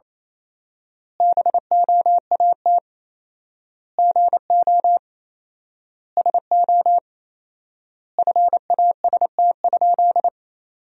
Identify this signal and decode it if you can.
Morse code, heard as a keyed tone, about 28 wpm, 700 Hz. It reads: BOAT GO SO FAST?
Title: 4x Word-Spacing - Sentences from Top 500 Words - 28wpm